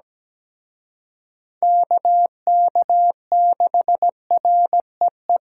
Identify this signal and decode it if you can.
Morse code, heard as a keyed tone, about 20 wpm, 700 Hz. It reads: KK6REE
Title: Speed Racing - USA 2x3 Call Signs 17wpm